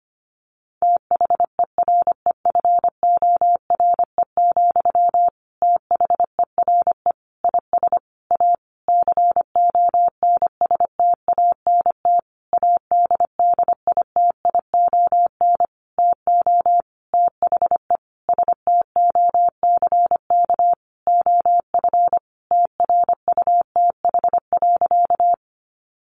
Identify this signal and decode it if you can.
Morse code, heard as a keyed tone, about 25 wpm, 700 Hz. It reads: THEREFORE, THERE IS A CONSTANT ADDITION TO THE STOCK OF TRUTH.
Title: Sentences from Top 1500 Words 25wpm